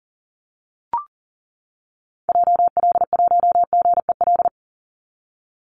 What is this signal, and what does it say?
Morse code, heard as a keyed tone, about 40 wpm, 700 Hz. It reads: JL1GEL
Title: Call Signs - 40wpm